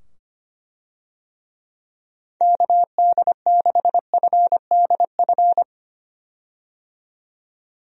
Morse code, heard as a keyed tone, about 25 wpm, 700 Hz. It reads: KD6FDF
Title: USA Call Signs - 6 Characters - 25wpm